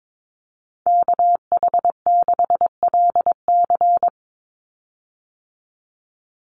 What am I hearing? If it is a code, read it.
Morse code, heard as a keyed tone, about 22 wpm, 700 Hz. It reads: KH6LC
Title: Call Signs Repeated 3x - 22wpm